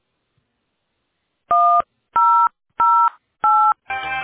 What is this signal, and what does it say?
DTMF, heard as telephone touch tones, tones of about 320 ms. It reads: 1008